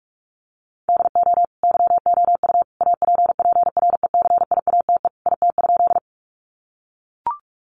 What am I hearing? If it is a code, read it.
Morse code, heard as a keyed tone, about 44 wpm, 700 Hz. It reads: DO YOU APPRECIATE IT?
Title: Sentences from Top 1300 Words 45wpm